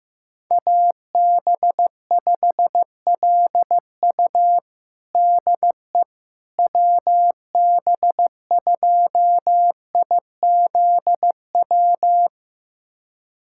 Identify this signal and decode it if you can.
Morse code, heard as a keyed tone, about 15 wpm, 700 Hz. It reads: AB5LU DE WB2IZW
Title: USA Callsign DE Callsign 15wpm